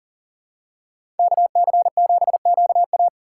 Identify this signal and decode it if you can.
Morse code, heard as a keyed tone, about 40 wpm, 700 Hz. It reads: KC7QA
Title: USA Call Signs - 5 Characters - 4x Character Spacing - 40wpm